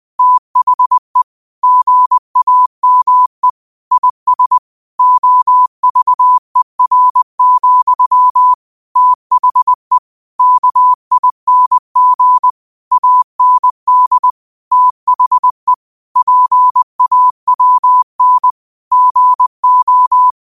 Morse code, heard as a keyed tone, about 20 wpm, 1000 Hz. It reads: THE GAME IS OVER, THE KING AND THE PAWN GO
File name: qotdfiles / 20wpm